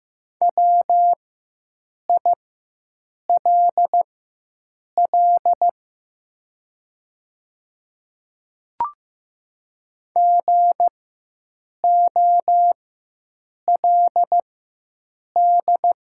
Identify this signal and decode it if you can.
Morse code, heard as a keyed tone, about 15 wpm, spaced as for 7 wpm, 690 Hz. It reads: WILL GOLD
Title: 4x Character Spacing - Top 500 Words - 15wpm